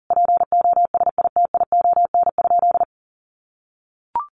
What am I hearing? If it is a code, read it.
Morse code, heard as a keyed tone, about 40 wpm, 700 Hz. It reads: POSITION?
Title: Sentences from Top 1400 Words 40wpm